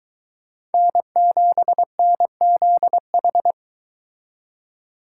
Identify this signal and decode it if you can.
Morse code, heard as a keyed tone, about 23 wpm, 700 Hz. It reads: N7NZH